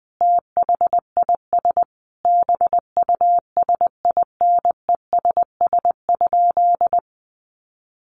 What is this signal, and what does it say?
Morse code, heard as a keyed tone, about 20 wpm, 700 Hz. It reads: THIS BUSINESS?
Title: Sentences from Top 600 Words 20wpm